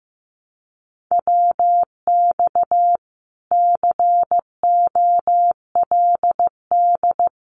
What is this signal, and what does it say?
Morse code, heard as a keyed tone, about 15 wpm, 700 Hz. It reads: WX COLD
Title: Common QSO Phrases Encore 15wpm